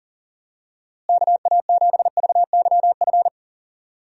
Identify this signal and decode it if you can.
Morse code, heard as a keyed tone, about 40 wpm, 700 Hz. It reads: KA7VYF
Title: USA Call Signs - 6 Characters - Repeated 3x - 40wpm